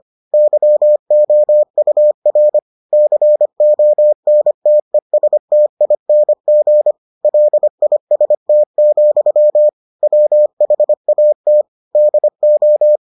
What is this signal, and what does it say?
Morse code, heard as a keyed tone, about 25 wpm, 600 Hz. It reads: YOUR CONTESTING LIST, WHAT DO